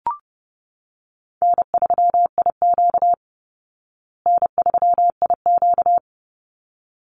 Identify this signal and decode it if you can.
Morse code, heard as a keyed tone, about 30 wpm, 700 Hz. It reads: N3IQ N3IQ
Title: Call Signs Repeated 3x - 30wpm